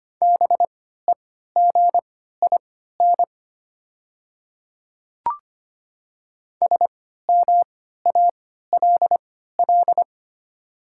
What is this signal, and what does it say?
Morse code, heard as a keyed tone, about 25 wpm, spaced as for 14 wpm, 690 Hz. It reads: BEGIN SMALL
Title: 3x Character Spacing - Top 500 Words - 25wpm